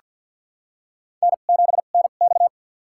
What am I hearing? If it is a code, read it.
Morse code, heard as a keyed tone, about 47 wpm, spaced as for 35 wpm, 700 Hz. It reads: N7NX
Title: USA Call Signs - 4 Characters - 2x Character Spacing - 50wpm